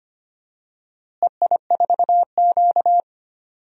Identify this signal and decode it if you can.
Morse code, heard as a keyed tone, about 25 wpm, 700 Hz. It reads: EI4Q
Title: DXCC-100 Call Signs - 4 Characters - Repeated 3x - 25wpm